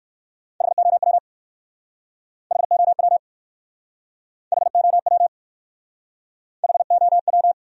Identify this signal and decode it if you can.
Morse code, heard as a keyed tone, about 60 wpm, 700 Hz. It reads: HOW HOW HOW HOW